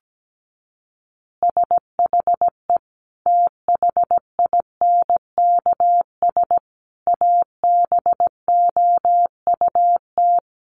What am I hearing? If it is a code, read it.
Morse code, heard as a keyed tone, about 17 wpm, 700 Hz. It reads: SHE THINKS ABOUT
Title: Easy Phrases 17wpm